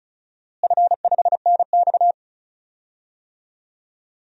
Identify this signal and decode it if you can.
Morse code, heard as a keyed tone, about 34 wpm, 700 Hz. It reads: F5NX
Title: DXCC-100 Call Signs - 4 Characters - Repeated 3x - 35wpm